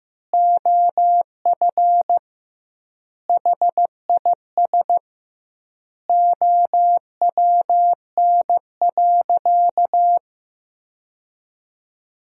Morse code, heard as a keyed tone, about 15 wpm, 700 Hz. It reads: OF HIS OWN.
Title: 2x Word-Spacing - Sentences from Top 300 Words - 15wpm